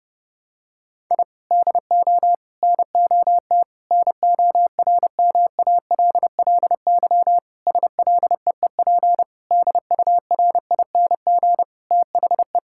Morse code, heard as a keyed tone, about 28 wpm, 700 Hz. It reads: I DO NOT NORMALLY SLEEP DURING THE
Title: Sentences from Top 1000 Words - Review Part 6 - 30wpm